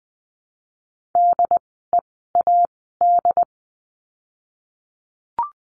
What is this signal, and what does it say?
Morse code, heard as a keyed tone, about 20 wpm, spaced as for 15 wpm, 700 Hz. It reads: DEAD
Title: 075 Lesson 13f - Letter L - Words 20wpm